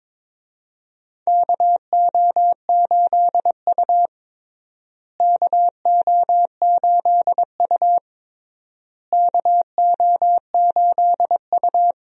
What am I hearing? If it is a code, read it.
Morse code, heard as a keyed tone, about 22 wpm, 700 Hz. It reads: KO8U KO8U KO8U
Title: USA Call Signs - 4 Characters - Repeated 3x - 22wpm